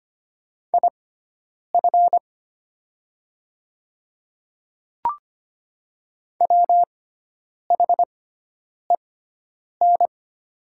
Morse code, heard as a keyed tone, about 25 wpm, spaced as for 9 wpm, 700 Hz. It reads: IF WHEN
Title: 6x Character Spacing - Top 100 Words - 25wpm